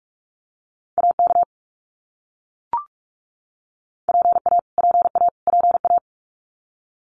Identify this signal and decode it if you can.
Morse code, heard as a keyed tone, about 44 wpm, 700 Hz. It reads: AK PA PA PA
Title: US States Repeated 3x 45wpm